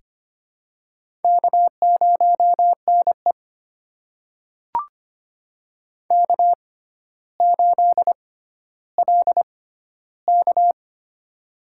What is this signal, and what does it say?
Morse code, heard as a keyed tone, about 25 wpm, spaced as for 13 wpm, 700 Hz. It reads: K0NE K8LK
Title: USA Call Signs - 4 Characters - 6x Character Spacing - 25wpm